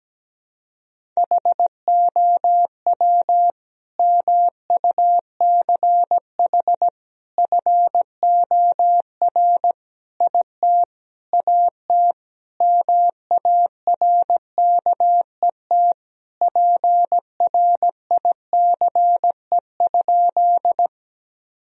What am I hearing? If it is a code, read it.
Morse code, heard as a keyed tone, about 17 wpm, 700 Hz. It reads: HOW MUCH FOR IT AT MARKET PRICE?
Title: Sentences from Top 1000 Words - Review Part 3 - 17wpm